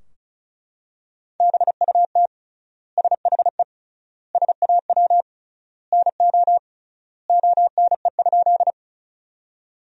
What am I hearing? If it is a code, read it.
Morse code, heard as a keyed tone, about 34 wpm, 700 Hz. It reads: BUT SHE SAW NO ONE?
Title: 3x Word-Spacing - Sentences from Top 300 Words - 35wpm